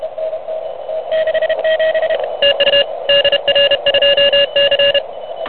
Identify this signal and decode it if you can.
Morse code, heard as a keyed tone, about 30 wpm, 2800 Hz, with noise. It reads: TU DR2C
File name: truSDX / MWC29.8